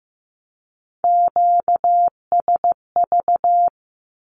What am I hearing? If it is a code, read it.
Morse code, heard as a keyed tone, about 15 wpm, 700 Hz. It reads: QSV
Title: Q-Codes with Spoken Meaning - 15wpm